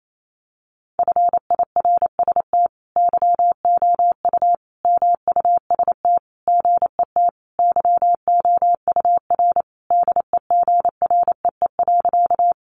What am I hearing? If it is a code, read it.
Morse code, heard as a keyed tone, about 28 wpm, 700 Hz. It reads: FIRST YOU MUST GET YOUR DEGREE.
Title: Sentences from Top 1500 Words 28wpm